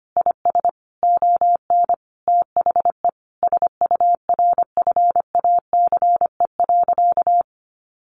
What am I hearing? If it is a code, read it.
Morse code, heard as a keyed tone, about 25 wpm, 700 Hz. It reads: IS ON THE SURFACE.
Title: Sentences from Top 600 Words 25wpm